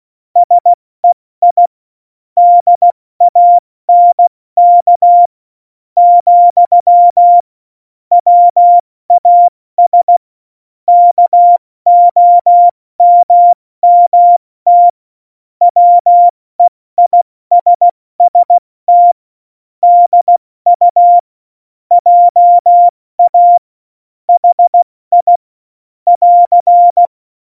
Morse code, heard as a keyed tone, about 16 wpm, 700 Hz. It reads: SEI DANK , WAS KOMMT WEISST DU JA HI <AR>
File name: Morsekurs / Witze / Witz 26 Mai, 2020 70 F80